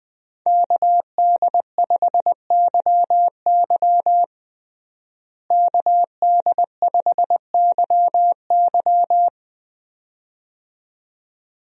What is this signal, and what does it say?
Morse code, heard as a keyed tone, about 20 wpm, 700 Hz. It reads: KD5YY KD5YY
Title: USA Call Signs - 5 Characters - Repeated 2x - 20wpm